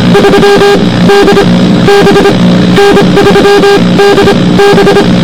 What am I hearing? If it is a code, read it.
Morse code, heard as a keyed tone, about 25 wpm, spaced as for 18 wpm, 400 Hz, with noise. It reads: 3DB N3DB